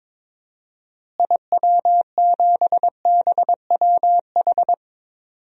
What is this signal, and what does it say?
Morse code, heard as a keyed tone, about 22 wpm, 700 Hz. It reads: IW7BWH
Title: DXCC-100 Call Signs - 6 Characters - Repeated 2x - 22wpm